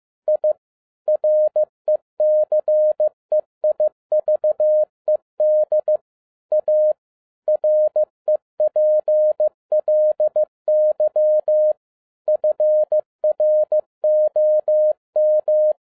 Morse code, heard as a keyed tone, about 15 wpm, 610 Hz. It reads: I RECEIVED A REPLY FROM
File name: 2011-10-mp3 / n07-A DOT 15wpm.mp3